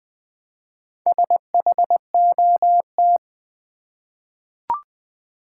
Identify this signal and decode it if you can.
Morse code, heard as a keyed tone, about 21 wpm, 700 Hz. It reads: SHOT
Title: Four Letter Words - 20wpm